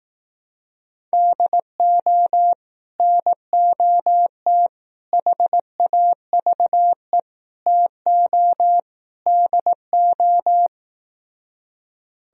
Morse code, heard as a keyed tone, about 18 wpm, 700 Hz. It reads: DO NOT HAVE TO DO
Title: Speed-Racing - Sets of 5 Words 15wpm